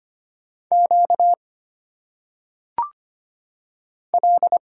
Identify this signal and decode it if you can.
Morse code, heard as a keyed tone, about 25 wpm, 700 Hz. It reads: Q L
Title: Single Letters - Phonetic Pronunciation 25wpm